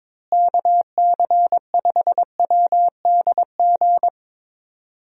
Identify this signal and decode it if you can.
Morse code, heard as a keyed tone, about 22 wpm, 700 Hz. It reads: KC5WDG
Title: USA Call Signs - 6 Characters - Repeated 3x - 22wpm